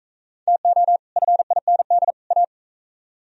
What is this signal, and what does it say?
Morse code, heard as a keyed tone, about 39 wpm, 700 Hz. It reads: TO FIND A